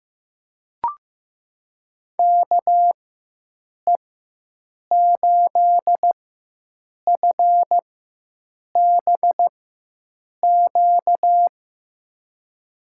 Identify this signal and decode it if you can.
Morse code, heard as a keyed tone, about 16 wpm, spaced as for 7 wpm, 700 Hz. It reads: KE8FBQ
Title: USA Call Signs - 6 Characters - 4x Character Spacing - 15wpm